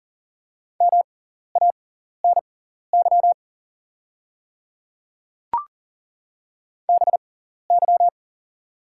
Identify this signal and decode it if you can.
Morse code, heard as a keyed tone, about 40 wpm, spaced as for 14 wpm, 700 Hz. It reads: MANY BY
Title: 6x Character Spacing - Top 100 Words - 40wpm